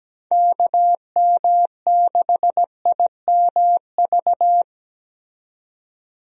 Morse code, heard as a keyed tone, about 17 wpm, 700 Hz. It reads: KM6IMV
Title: USA Call Signs - 6 Characters - 17wpm